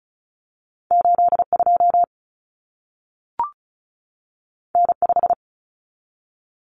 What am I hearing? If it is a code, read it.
Morse code, heard as a keyed tone, about 36 wpm, 700 Hz. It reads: 82 N5
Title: Common Cut Numbers - 2-Digit 35wpm